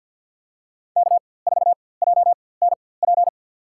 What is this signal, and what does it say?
Morse code, heard as a keyed tone, about 53 wpm, spaced as for 24 wpm, 700 Hz. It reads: K4JNP
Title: USA Call Signs - 5 Characters - 4x Character Spacing - 50wpm